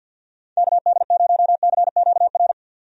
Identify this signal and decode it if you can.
Morse code, heard as a keyed tone, about 51 wpm, 700 Hz. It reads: KD0CQR
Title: USA Call Signs - 6 Characters - 50wpm